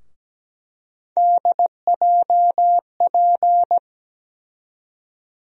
Morse code, heard as a keyed tone, about 17 wpm, 700 Hz. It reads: DJP
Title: USA Call Sign Postfixes - 3 Letters 17wpm